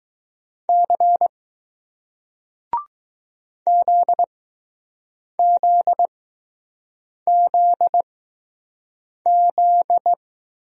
Morse code, heard as a keyed tone, about 20 wpm, spaced as for 6 wpm, 700 Hz. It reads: C ZZZZ